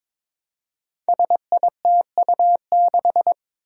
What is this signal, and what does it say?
Morse code, heard as a keyed tone, about 23 wpm, 700 Hz. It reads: SITU6
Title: Five Challenging Letter-Number Combinations 22wpm